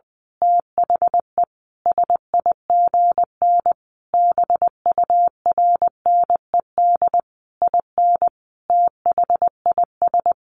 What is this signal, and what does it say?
Morse code, heard as a keyed tone, about 20 wpm, 700 Hz. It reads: THE SIGN BURNED IN THIS
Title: A Sound Of Thunder 20wpm